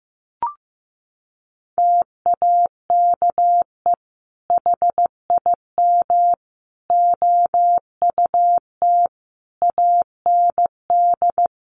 Morse code, heard as a keyed tone, about 15 wpm, 700 Hz. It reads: TAKE HIM OUT AND
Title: Sentences from Top 1200 Words 15wpm